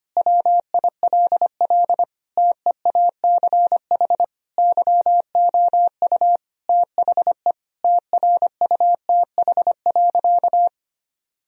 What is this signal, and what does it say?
Morse code, heard as a keyed tone, about 25 wpm, 700 Hz. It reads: WILL TEACH YOU THE TRUTH.
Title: Sentences from Top 1000 Words - Review Part 2 - 25wpm